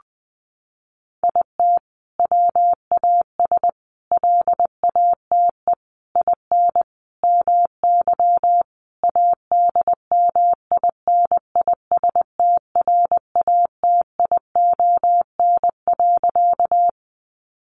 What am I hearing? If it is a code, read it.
Morse code, heard as a keyed tone, about 20 wpm, 700 Hz. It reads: IT WAS LATE IN MY ADMINISTRATION.
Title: Sentences from Top 1300 Words 20wpm